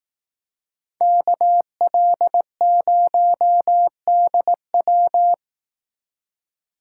Morse code, heard as a keyed tone, about 18 wpm, 700 Hz. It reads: KL0DW
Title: Speed Racing - USA Call Signs 15wpm